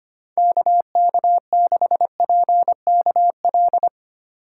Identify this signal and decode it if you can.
Morse code, heard as a keyed tone, about 25 wpm, 700 Hz. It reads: KK6PKL
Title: USA Call Signs - 6 Characters - 3x Character Spacing - 25wpm